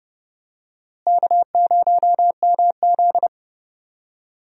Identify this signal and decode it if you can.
Morse code, heard as a keyed tone, about 32 wpm, 700 Hz. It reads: K0MZ
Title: USA Call Signs - 4 Characters - Repeated 3x - 30wpm